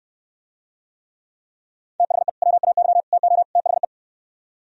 Morse code, heard as a keyed tone, about 68 wpm, 700 Hz. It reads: THE WAY TO THE